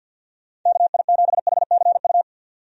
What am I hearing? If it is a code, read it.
Morse code, heard as a keyed tone, about 51 wpm, 700 Hz. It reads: KI7HCU
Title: USA Call Signs - 6 Characters - 50wpm